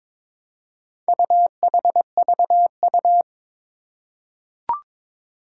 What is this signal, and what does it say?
Morse code, heard as a keyed tone, about 22 wpm, 700 Hz. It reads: UHVU
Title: Four Challenging Letter-Number Combinations 22wpm